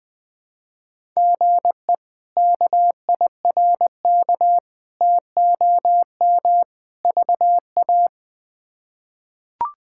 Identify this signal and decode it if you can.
Morse code, heard as a keyed tone, about 20 wpm, 700 Hz. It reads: GE KIRK TOM VA
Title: K1USN SST QSO Practice - 20wpm